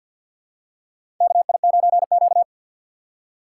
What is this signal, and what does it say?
Morse code, heard as a keyed tone, about 52 wpm, 700 Hz. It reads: KI9Q
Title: USA Call Signs - 4 Characters - 6x Character Spacing - 50wpm